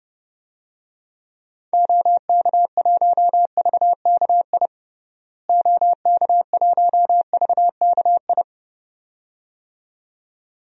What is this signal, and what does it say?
Morse code, heard as a keyed tone, about 30 wpm, 700 Hz. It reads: OK1VKI OK1VKI
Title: DXCC-100 Call Signs - 6 Characters - Repeated 2x - 30wpm